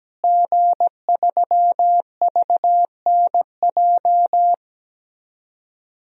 Morse code, heard as a keyed tone, about 17 wpm, 700 Hz. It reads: G3VNJ
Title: DXCC-100 Call Signs - 5 Characters - Repeated 3x - 17wpm